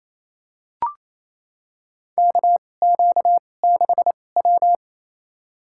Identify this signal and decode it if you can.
Morse code, heard as a keyed tone, about 28 wpm, spaced as for 20 wpm, 700 Hz. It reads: KQ6W
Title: USA Call Signs - 4 Characters - 2x Character Spacing - 28wpm